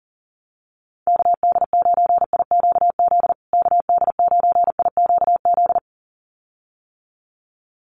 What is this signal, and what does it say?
Morse code, heard as a keyed tone, about 40 wpm, 700 Hz. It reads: KD9IQZ KD9IQZ
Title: USA Call Signs Repeated 2x 40wpm